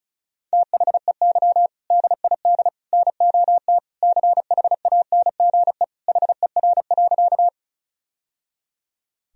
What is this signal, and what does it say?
Morse code, heard as a keyed tone, about 36 wpm, 700 Hz. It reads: THEY DID NOT CHANGE HER.
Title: Sentences from Top 1000 Words - Review Part 1 - 35wpm